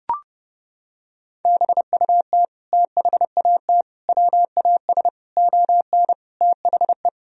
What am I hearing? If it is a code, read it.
Morse code, heard as a keyed tone, about 30 wpm, 700 Hz. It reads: BUT THAT WAS ON THE
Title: Sentences from Top 1000 Words - Review Part 1 - 30wpm